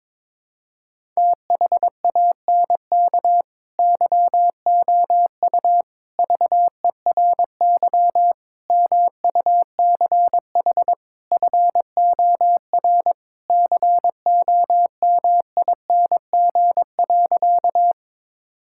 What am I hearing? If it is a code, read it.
Morse code, heard as a keyed tone, about 22 wpm, 700 Hz. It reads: THANK YOU VERY MUCH FOR COMING.
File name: practice / 22wpm / Sentences with 4-Word Phrases - 22wpm